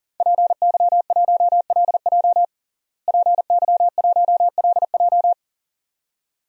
Morse code, heard as a keyed tone, about 40 wpm, 700 Hz. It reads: PY1LJ PY1LJ
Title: DXCC-100 Call Signs - 5 Characters - Repeated 2x - 40wpm